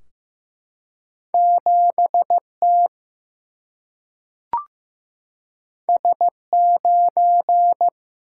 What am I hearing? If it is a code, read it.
Morse code, heard as a keyed tone, about 15 wpm, 700 Hz. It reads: 7 T S 9
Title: Single-Letter Single-Number 15wpm